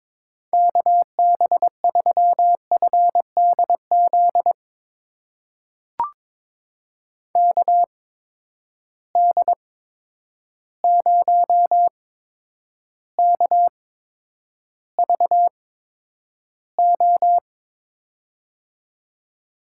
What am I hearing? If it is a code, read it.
Morse code, heard as a keyed tone, about 22 wpm, spaced as for 9 wpm, 700 Hz. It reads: KB3FDZ KD0KVO
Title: USA Call Signs - 6 Characters - 8x Character Spacing - 22wpm